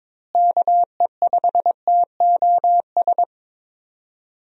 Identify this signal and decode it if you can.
Morse code, heard as a keyed tone, about 22 wpm, 700 Hz. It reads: KE5TOS
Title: USA Call Signs - 6 Characters - Repeated 2x - 22wpm